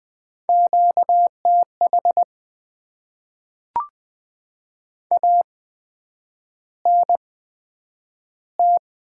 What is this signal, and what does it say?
Morse code, heard as a keyed tone, about 20 wpm, spaced as for 9 wpm, 700 Hz. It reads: QTH ANT
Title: Top 20 QSO Elements - 8x Character Spacing - 20wpm